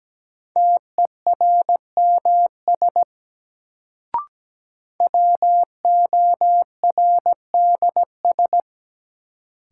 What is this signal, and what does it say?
Morse code, heard as a keyed tone, about 17 wpm, 700 Hz. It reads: TERMS WORDS
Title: Sets of 3 Words - New Word Drill - 17wpm